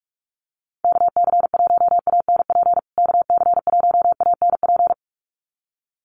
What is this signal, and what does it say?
Morse code, heard as a keyed tone, about 44 wpm, 700 Hz. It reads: KC1ANP KC1ANP
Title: USA Call Signs Repeated 2x 45wpm